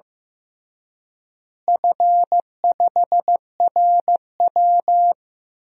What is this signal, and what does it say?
Morse code, heard as a keyed tone, about 16 wpm, 700 Hz. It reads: F5RW
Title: DXCC-100 Call Signs - 4 Characters - Repeated 3x - 15wpm